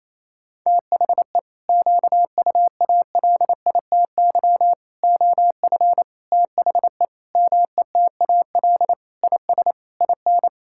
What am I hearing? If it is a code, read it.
Morse code, heard as a keyed tone, about 28 wpm, 700 Hz. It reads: THE QUALITY OF THE METAL IS IN
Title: Sentences from Top 1000 Words - Review Part 6 - 28wpm